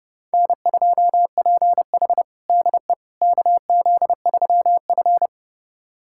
Morse code, heard as a keyed tone, about 31 wpm, 700 Hz. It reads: N2PH DE KZ3F
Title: USA Callsign DE Callsign - 4 Characters - 30wpm